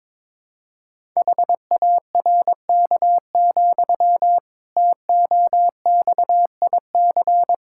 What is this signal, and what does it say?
Morse code, heard as a keyed tone, about 22 wpm, 700 Hz. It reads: HARK, TOXIC
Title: Pangram Challenge v2 - 22wpm